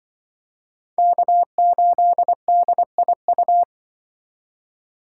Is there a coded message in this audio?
Yes: Morse code from 1.0 to 3.6 s, heard as a keyed tone, about 26 wpm, 700 Hz. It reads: K8DIU